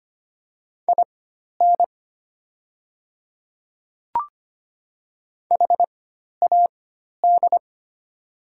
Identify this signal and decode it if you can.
Morse code, heard as a keyed tone, about 25 wpm, spaced as for 12 wpm, 700 Hz. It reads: IN HAD
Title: 4x Character Spacing - Top 200 Words - 25wpm